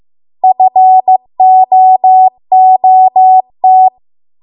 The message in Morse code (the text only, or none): FOOT